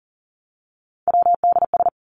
Morse code, heard as a keyed tone, about 40 wpm, 700 Hz. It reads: WDS
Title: Abbreviations 40wpm